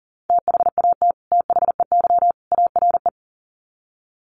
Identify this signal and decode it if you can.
Morse code, heard as a keyed tone, about 40 wpm, 700 Hz. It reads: THAT THEY ARE
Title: Sets of 3 Words 40wpm